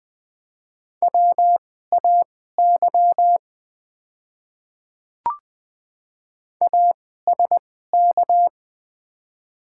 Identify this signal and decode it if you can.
Morse code, heard as a keyed tone, about 20 wpm, spaced as for 14 wpm, 700 Hz. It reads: WAY ASK
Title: 2x Character Spacing - Top 300 Words - 20wpm